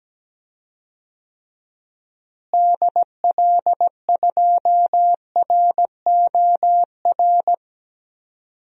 Morse code, heard as a keyed tone, about 17 wpm, 700 Hz. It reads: DL2ROR
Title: DXCC-100 Call Signs - 6 Characters - Repeated 2x - 17wpm